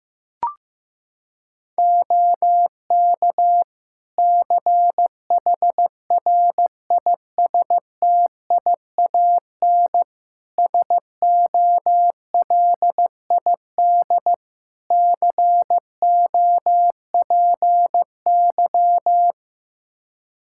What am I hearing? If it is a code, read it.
Morse code, heard as a keyed tone, about 15 wpm, 700 Hz. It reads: OK CHRISTIAN SOLID COPY
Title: Practice QSO - Exchange 3 - More Info - 15wpm